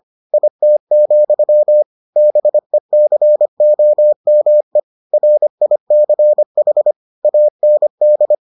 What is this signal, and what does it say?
Morse code, heard as a keyed tone, about 25 wpm, 600 Hz. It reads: IT, BECOME RICH AND